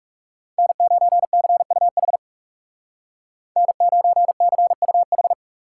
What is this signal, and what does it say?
Morse code, heard as a keyed tone, about 36 wpm, 700 Hz. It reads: N9CUH N9CUH
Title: Speed Racing - USA 1x3 Call Signs 30wpm